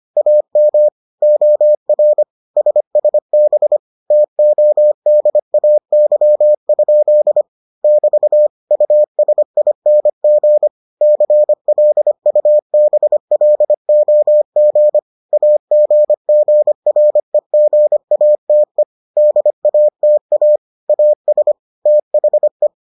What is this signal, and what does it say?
Morse code, heard as a keyed tone, about 25 wpm, 600 Hz. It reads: AM OR SSB TODAY? <BT> USING CLUBLOG AGGREGATE DATA AS THE